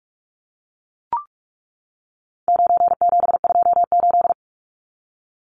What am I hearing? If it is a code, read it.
Morse code, heard as a keyed tone, about 44 wpm, 700 Hz. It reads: 9728
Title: Numbers - 4-Digit 45wpm